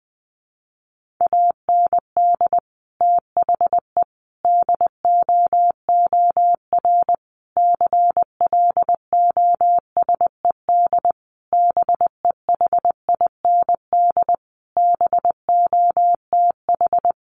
Morse code, heard as a keyed tone, about 20 wpm, 700 Hz. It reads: AND THE DOOR CLOSED BEHIND BOTH
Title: Sentences from Top 1400 Words 20wpm